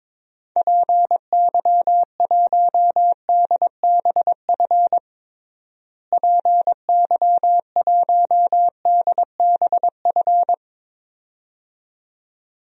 Morse code, heard as a keyed tone, about 22 wpm, 700 Hz. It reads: PY1DBF PY1DBF
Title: DXCC-100 Call Signs - 6 Characters - Repeated 2x - 22wpm